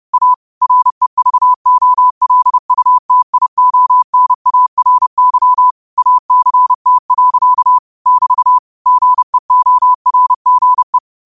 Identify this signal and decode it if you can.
Morse code, heard as a keyed tone, about 30 wpm, 1000 Hz. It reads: A REVOLUTIONARY ACT. <BT> GEORGE